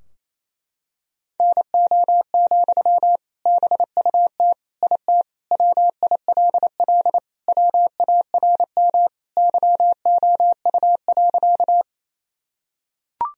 Morse code, heard as a keyed tone, about 28 wpm, 700 Hz. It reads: NO, BUT IT WILL WARM YOU.
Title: Sentences from Top 1000 Words - Review Part 2 - 28wpm